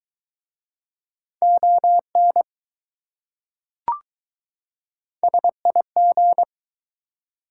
Morse code, heard as a keyed tone, about 21 wpm, 700 Hz. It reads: ON SIG